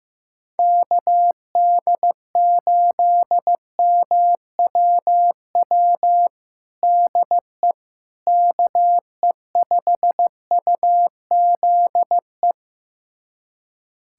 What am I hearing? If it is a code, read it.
Morse code, heard as a keyed tone, about 15 wpm, 700 Hz. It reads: KD8MWW DE KE5UZE
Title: USA Callsign DE Callsign - 6 Characters - 15wpm